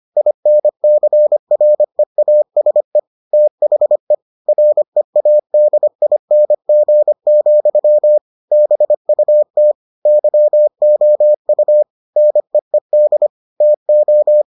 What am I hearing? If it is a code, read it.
Morse code, heard as a keyed tone, about 25 wpm, 600 Hz. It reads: INCREASE THE READING, BUT YOU NEED TO